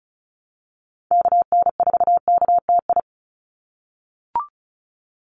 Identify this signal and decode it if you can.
Morse code, heard as a keyed tone, about 34 wpm, 700 Hz. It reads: KN4KTI
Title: USA Call Signs 35wpm